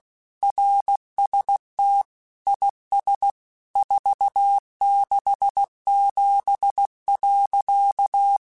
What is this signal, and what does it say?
Morse code, heard as a keyed tone, about 16 wpm, 800 Hz. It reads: RST IS 467.
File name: QSOs / QSOs-mp3s / WA7QOX 2@15